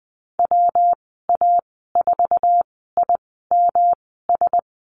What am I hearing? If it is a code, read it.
Morse code, heard as a keyed tone, about 20 wpm, spaced as for 14 wpm, 700 Hz. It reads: WA4IMS
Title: 154 Lesson 25g - Letter p - Call Signs 20wpm